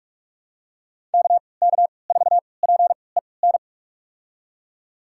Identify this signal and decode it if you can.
Morse code, heard as a keyed tone, about 43 wpm, spaced as for 25 wpm, 700 Hz. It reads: KK4PEN
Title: USA Call Signs - 6 Characters - 3x Character Spacing - 45wpm